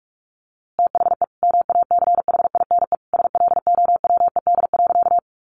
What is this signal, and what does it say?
Morse code, heard as a keyed tone, about 44 wpm, 700 Hz. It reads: THE MACHINE SLOWED.
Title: A Sound Of Thunder 45wpm